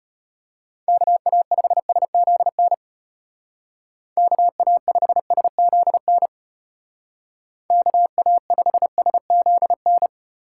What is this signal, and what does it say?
Morse code, heard as a keyed tone, about 32 wpm, 700 Hz. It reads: KA5SZN KA5SZN KA5SZN